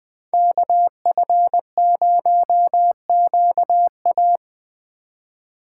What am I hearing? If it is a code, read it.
Morse code, heard as a keyed tone, about 20 wpm, 700 Hz. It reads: KF0QA
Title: USA Call Signs - 5 Characters - 2x Character Spacing - 20wpm